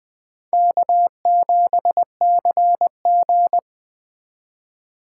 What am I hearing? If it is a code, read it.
Morse code, heard as a keyed tone, about 20 wpm, 700 Hz. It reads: K7CG
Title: Speed Racing - USA 1x2 Call Signs 17wpm